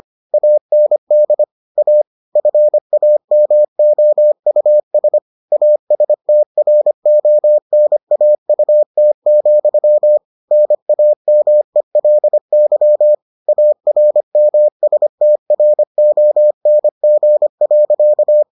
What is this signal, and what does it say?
Morse code, heard as a keyed tone, about 25 wpm, 600 Hz. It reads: AND A FAMOUS ASTRONAUT, NAMELY ARMSTRONG.